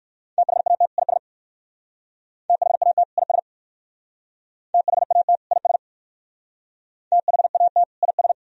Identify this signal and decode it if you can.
Morse code, heard as a keyed tone, about 54 wpm, 700 Hz. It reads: THAT IS THAT IS THAT IS THAT IS
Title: Speed-Racing - Sets of 2 Words 45wpm